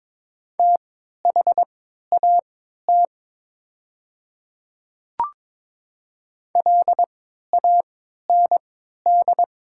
Morse code, heard as a keyed tone, about 22 wpm, spaced as for 12 wpm, 700 Hz. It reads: THAT LAND
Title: 3x Character Spacing - Top 200 Words - 22wpm